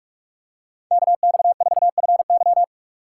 Morse code, heard as a keyed tone, about 43 wpm, 700 Hz. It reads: KX4FY
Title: USA Call Signs - 5 Characters - 3x Character Spacing - 45wpm